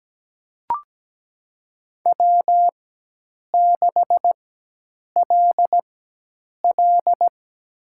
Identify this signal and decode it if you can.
Morse code, heard as a keyed tone, about 17 wpm, spaced as for 8 wpm, 700 Hz. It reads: W6LL
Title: USA Call Signs - 4 Characters - 4x Character Spacing - 17wpm